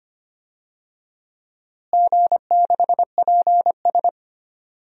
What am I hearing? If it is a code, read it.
Morse code, heard as a keyed tone, about 25 wpm, 700 Hz. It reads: G6PS
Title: DXCC-100 Call Signs - 4 Characters - Repeated 2x - 25wpm